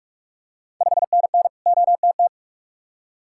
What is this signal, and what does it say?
Morse code, heard as a keyed tone, about 45 wpm, 690 Hz. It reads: 5NN OTT
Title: Ontario QSO Party Exchange 45wpm